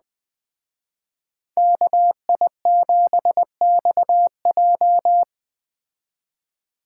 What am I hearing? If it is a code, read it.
Morse code, heard as a keyed tone, about 20 wpm, 700 Hz. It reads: KI7XJ
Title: Speed Racing - USA 2x2 Call Signs 17wpm